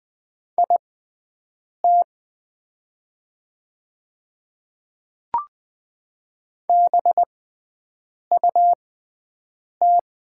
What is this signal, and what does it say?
Morse code, heard as a keyed tone, about 20 wpm, spaced as for 7 wpm, 700 Hz. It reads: IT BUT
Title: 6x Character Spacing - Top 300 Words - 20wpm